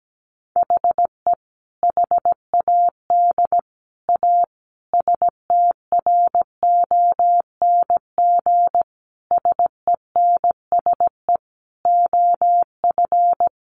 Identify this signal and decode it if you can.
Morse code, heard as a keyed tone, about 17 wpm, 700 Hz. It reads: HE HAD A STRONG SENSE OF
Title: Sentences from Top 1500 Words 17wpm